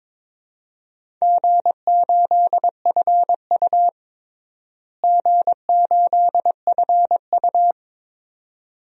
Morse code, heard as a keyed tone, about 22 wpm, 700 Hz. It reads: G8FU G8FU
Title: DXCC-100 Call Signs - 4 Characters - Repeated 2x - 22wpm